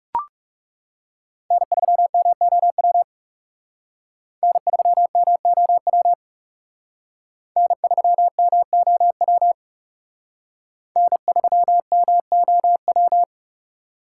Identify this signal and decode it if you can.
Morse code, heard as a keyed tone, about 37 wpm, 700 Hz. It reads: N3MOW N3MOW N3MOW N3MOW
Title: Speed Racing - USA 1x3 Call Signs 30wpm